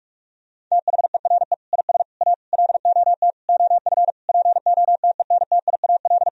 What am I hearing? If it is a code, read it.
Morse code, heard as a keyed tone, about 45 wpm, 700 Hz. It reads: THERE IS A LOT OF POTENTIAL
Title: Sentences from Top 1000 Words - Review Part 4 - 45wpm